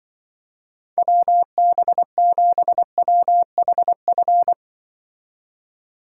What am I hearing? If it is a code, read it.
Morse code, heard as a keyed tone, about 24 wpm, 700 Hz. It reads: WB7WHF
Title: Speed Racing - USA 2x3 Call Signs 20wpm